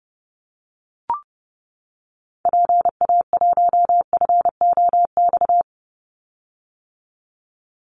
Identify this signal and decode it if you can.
Morse code, heard as a keyed tone, about 29 wpm, 700 Hz. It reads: PA1FOX
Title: Call Signs - 30wpm